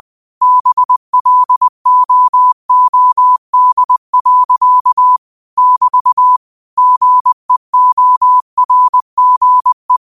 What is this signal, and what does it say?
Morse code, heard as a keyed tone, about 20 wpm, 1000 Hz. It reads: BLOOD. <BT> GEORGE
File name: qotdfiles / 20wpm